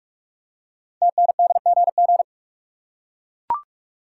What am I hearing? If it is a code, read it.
Morse code, heard as a keyed tone, about 44 wpm, 700 Hz. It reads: TNDGG
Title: Code Groups - 5 Letters - 45wpm